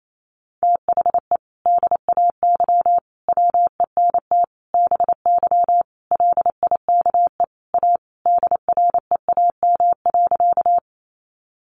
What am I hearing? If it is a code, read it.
Morse code, heard as a keyed tone, about 28 wpm, 700 Hz. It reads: THE DAY WENT BY LIKE A DREAM.
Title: Sentences from Top 1500 Words 28wpm